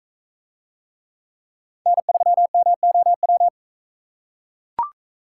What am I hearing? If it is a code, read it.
Morse code, heard as a keyed tone, about 42 wpm, 700 Hz. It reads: N3MOW